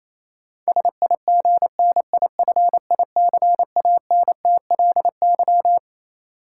Morse code, heard as a keyed tone, about 28 wpm, 700 Hz. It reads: SIGNIFICANTLY
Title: Sentences from Top 1500 Words - New Word Drill - 28wpm